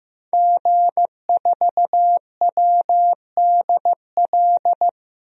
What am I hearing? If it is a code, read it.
Morse code, heard as a keyed tone, about 15 wpm, 700 Hz. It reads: G4WDL
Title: DXCC-100 Call Signs - 5 Characters - 15wpm